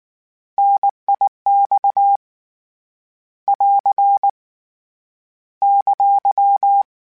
Morse code, heard as a keyed tone, about 19 wpm, 800 Hz. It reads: NIX <AR> !
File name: Contrasts / Contrasts-mp3s / 3LtrContrasts @18